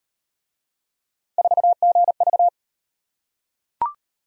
Morse code, heard as a keyed tone, about 36 wpm, 700 Hz. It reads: 4GV